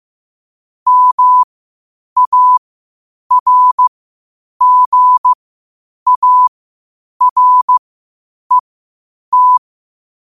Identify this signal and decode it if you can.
Morse code, heard as a keyed tone, about 15 wpm, spaced as for 8 wpm, 1000 Hz. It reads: MARGARET
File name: qotdfiles / 8wpm